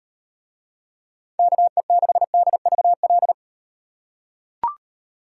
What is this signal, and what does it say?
Morse code, heard as a keyed tone, about 38 wpm, 700 Hz. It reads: KE6DVL